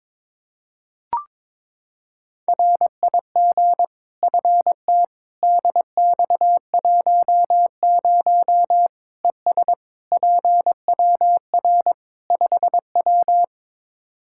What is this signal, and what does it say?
Morse code, heard as a keyed tone, about 22 wpm, 700 Hz. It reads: RIG FT DX10 ES PWR 5W
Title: Practice QSO - Exchange 2 - Info - 22wpm